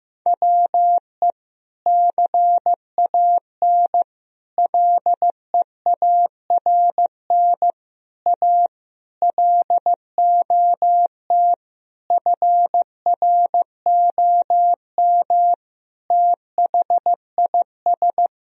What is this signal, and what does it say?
Morse code, heard as a keyed tone, about 16 wpm, 700 Hz. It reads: WE CAN LEARN A LOT FROM THIS
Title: Sentences from Top 1000 Words - Review Part 2 - 15wpm